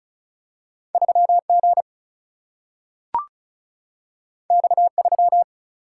Morse code, heard as a keyed tone, about 36 wpm, 700 Hz. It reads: 3 G X 3
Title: Single-Letter Single-Number 35wpm